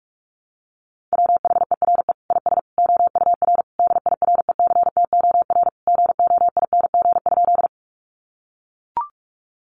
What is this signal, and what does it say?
Morse code, heard as a keyed tone, about 44 wpm, 700 Hz. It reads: WHERE IS OUR DIRECTOR GOING?
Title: Sentences from Top 1300 Words 45wpm